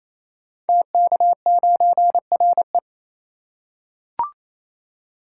Morse code, heard as a keyed tone, about 28 wpm, 700 Hz. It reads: TK9RE
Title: Code Groups - 5 Letter:Numbers - 28wpm